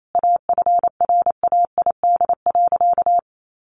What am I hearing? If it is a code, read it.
Morse code, heard as a keyed tone, about 28 wpm, 700 Hz. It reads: AFRAID.
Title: Sentences from Top 1500 Words 28wpm